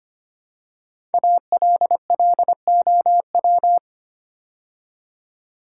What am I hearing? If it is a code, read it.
Morse code, heard as a keyed tone, about 25 wpm, 700 Hz. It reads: ALLOW
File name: 5 Letter Words - First 3 Letters Given - 25wpm